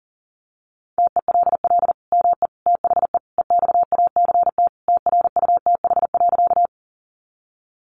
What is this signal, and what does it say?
Morse code, heard as a keyed tone, about 40 wpm, 700 Hz. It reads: TELL ME THE EXACT TRUTH.
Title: Sentences from Top 1300 Words 40wpm